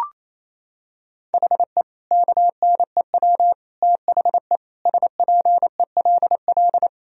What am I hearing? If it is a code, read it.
Morse code, heard as a keyed tone, about 28 wpm, 700 Hz. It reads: HE KNEW THE SPELL
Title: Sentences from Top 1000 Words - Review Part 1 - 28wpm